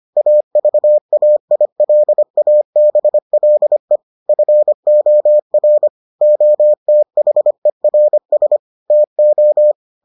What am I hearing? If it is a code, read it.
Morse code, heard as a keyed tone, about 25 wpm, 600 Hz. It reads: AVAILABLE FOR OTHERS TO